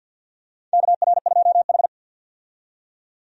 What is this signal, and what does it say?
Morse code, heard as a keyed tone, about 47 wpm, 700 Hz. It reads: KR2H
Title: USA Call Signs - 4 Characters - 50wpm